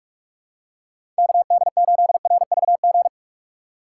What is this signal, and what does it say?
Morse code, heard as a keyed tone, about 46 wpm, 700 Hz. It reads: KD8RVG
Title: USA Call Signs - 4x Character Spacing - 45wpm